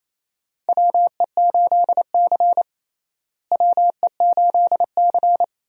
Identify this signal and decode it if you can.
Morse code, heard as a keyed tone, about 28 wpm, 700 Hz. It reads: WE8C WE8C
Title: USA Call Signs - 4 Characters - Repeated 2x - 28wpm